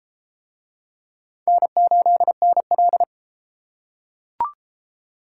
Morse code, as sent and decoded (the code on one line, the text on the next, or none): -. ---.. -. .-..
N8NL